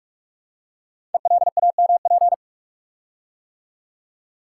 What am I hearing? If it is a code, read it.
Morse code, heard as a keyed tone, about 44 wpm, 690 Hz. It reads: ELAMP
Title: Code Groups - 5 Letters - 45wpm